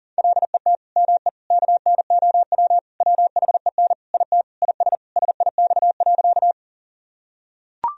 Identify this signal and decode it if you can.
Morse code, heard as a keyed tone, about 40 wpm, 700 Hz. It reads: LET ME KNOW WHEN IT IS SIX.
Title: Sentences from Top 1000 Words - Review Part 2 - 40wpm